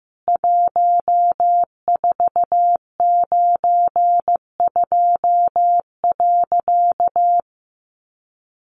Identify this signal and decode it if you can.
Morse code, heard as a keyed tone, about 15 wpm, 700 Hz. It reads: 1492.
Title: A Sound Of Thunder 15wpm